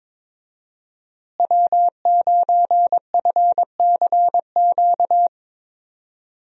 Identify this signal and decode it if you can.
Morse code, heard as a keyed tone, about 22 wpm, 700 Hz. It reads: W9FCQ
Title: USA Call Signs - 5 Characters - Repeated 3x - 22wpm